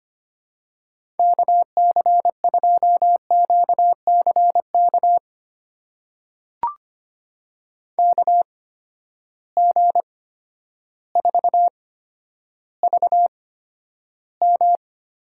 Morse code, heard as a keyed tone, about 25 wpm, spaced as for 11 wpm, 700 Hz. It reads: KC2QCK KG4VM
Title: USA Call Signs - 6 Characters - 8x Character Spacing - 25wpm